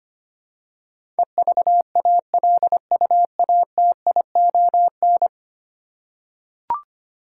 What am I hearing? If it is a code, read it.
Morse code, heard as a keyed tone, about 25 wpm, 700 Hz. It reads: EVALUATION
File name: Ten Letter Words - 25wpm